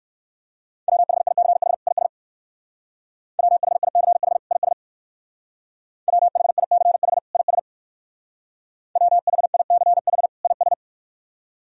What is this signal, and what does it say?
Morse code, heard as a keyed tone, about 56 wpm, 700 Hz. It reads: WHICH IS WHICH IS WHICH IS WHICH IS
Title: Speed-Racing - Sets of 2 Words 45wpm